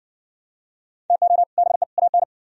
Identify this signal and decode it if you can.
Morse code, heard as a keyed tone, about 59 wpm, 700 Hz. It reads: TO BE AN